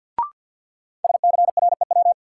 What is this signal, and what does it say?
Morse code, heard as a keyed tone, about 49 wpm, 700 Hz. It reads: SCREW
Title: Sentences from Top 1600 Words - New Word Drill Alt - 50wpm